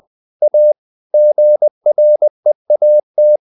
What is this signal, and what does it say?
Morse code, heard as a keyed tone, about 20 wpm, 600 Hz. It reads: A GREAT
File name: the old man and the sea part 2 20wpm